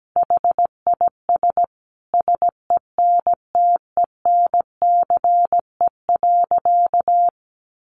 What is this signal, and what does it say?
Morse code, heard as a keyed tone, about 17 wpm, 700 Hz. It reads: HIS SENTENCE.
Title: Sentences from Top 1200 Words 17wpm